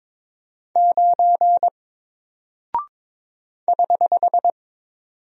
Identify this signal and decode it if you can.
Morse code, heard as a keyed tone, about 22 wpm, 700 Hz. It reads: 9 <HH>
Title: Single Character (Letter:Number:Punctuation:Prosign) Alt 22wpm